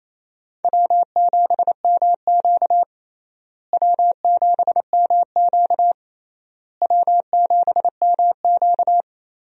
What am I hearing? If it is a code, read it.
Morse code, heard as a keyed tone, about 28 wpm, 700 Hz. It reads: W7MQ W7MQ W7MQ
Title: USA Call Signs - 4 Characters - Repeated 3x - 28wpm